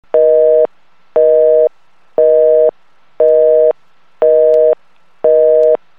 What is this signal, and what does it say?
Busy tone (North America), 480 + 620 Hz, with noise.
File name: Sound FX / House Phones / 00 - Busy Signal